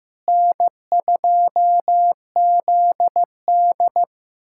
Morse code, heard as a keyed tone, about 15 wpm, 700 Hz. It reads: N2ZD
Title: USA Callsign DE Callsign - 4 Characters - 15wpm